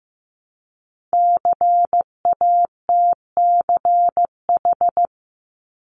Morse code, heard as a keyed tone, about 15 wpm, 700 Hz. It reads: CATCH
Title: Multi-letter Phonemes Repeated 3x 15wpm